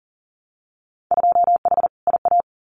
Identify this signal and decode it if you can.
Morse code, heard as a keyed tone, about 40 wpm, 700 Hz. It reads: 2H IA
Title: Winter Field Day Exchanges Encore 40wpm